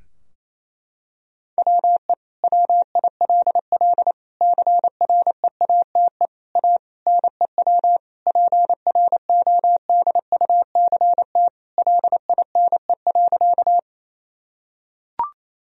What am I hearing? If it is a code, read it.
Morse code, heard as a keyed tone, about 28 wpm, 700 Hz. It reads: WE WILL CREATE A NEW PRODUCT LINE.
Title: Sentences from Top 1000 Words - Review Part 3 - 28wpm